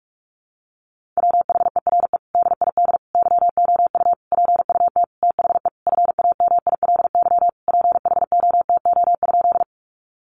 Morse code, heard as a keyed tone, about 44 wpm, 700 Hz. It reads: WHERE DID YOU PUT THE FAMILY PHOTO?
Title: Sentences from Top 1200 Words 45wpm